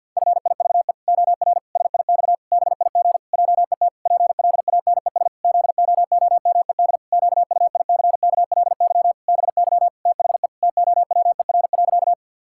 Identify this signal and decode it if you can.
Morse code, heard as a keyed tone, about 52 wpm, 700 Hz. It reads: FIVE OR SIX BIG JET PLANES ZOOMED QUICKLY BY THE TOWER.
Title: Pangram Challenge v2 - 50wpm